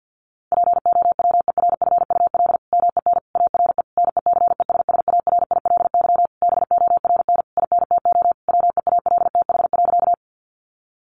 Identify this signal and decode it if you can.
Morse code, heard as a keyed tone, about 50 wpm, 700 Hz. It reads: POWERFUL MEN ARE NECESSARILY BORN INTO WEALTH.
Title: Sentences from Top 1400 Words 50wpm